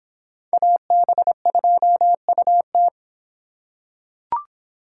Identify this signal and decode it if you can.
Morse code, heard as a keyed tone, about 24 wpm, 700 Hz. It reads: AB2UT